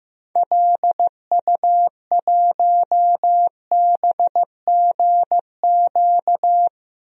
Morse code, heard as a keyed tone, about 15 wpm, 700 Hz. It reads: LU1BGQ
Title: DXCC-100 Call Signs - 6 Characters - 15wpm